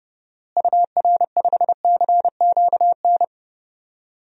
Morse code, heard as a keyed tone, about 31 wpm, 700 Hz. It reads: UR5CQN
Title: DXCC-100 Call Signs - 6 Characters - Repeated 3x - 30wpm